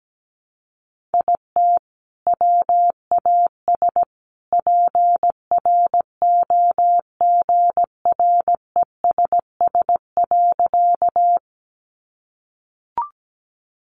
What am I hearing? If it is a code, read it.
Morse code, heard as a keyed tone, about 17 wpm, 700 Hz. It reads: IT WAS PROGRESS.
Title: Sentences from Top 1400 Words 17wpm